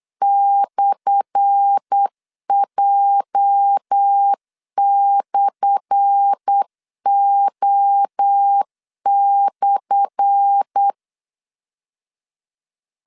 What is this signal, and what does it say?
Morse code, heard as a keyed tone, about 8 wpm, 800 Hz. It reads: /J/O/
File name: LEZIONI / 10 SIGNS / 10A SLASH 8wpm 3452